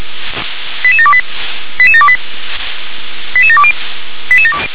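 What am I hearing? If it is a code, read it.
Five-tone selective call (ZVEI), heard as quick runs of five tones, tones of about 70 ms. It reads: 88728 88728 88720
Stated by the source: ZVEI five-tone call 88730 received by SDR.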